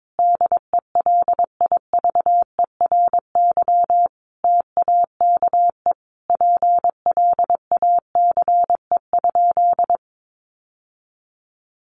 Morse code, heard as a keyed tone, about 22 wpm, 700 Hz. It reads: DELIVERY TAKE PLACE?
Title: Sentences from Top 1500 Words 22wpm